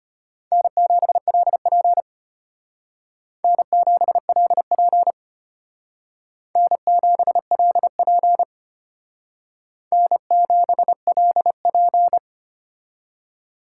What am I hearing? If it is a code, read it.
Morse code, heard as a keyed tone, about 32 wpm, 700 Hz. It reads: N7LP N7LP N7LP N7LP